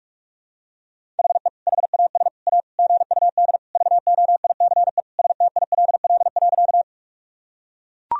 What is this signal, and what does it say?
Morse code, heard as a keyed tone, about 44 wpm, 700 Hz. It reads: HE HAS A GUD VOICE STILL.
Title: Sentences from Top 500 Words and Abbreviations 45wpm